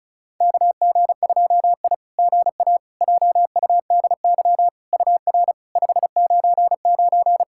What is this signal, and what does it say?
Morse code, heard as a keyed tone, about 34 wpm, 690 Hz. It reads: KG2I GA JUDY UR 599
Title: SOTA QSO Practice 35wpm